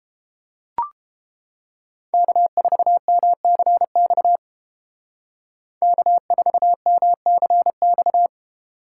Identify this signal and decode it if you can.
Morse code, heard as a keyed tone, about 30 wpm, 700 Hz. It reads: K4MCX K4MCX